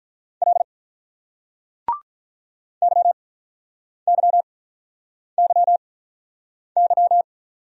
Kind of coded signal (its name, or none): Morse code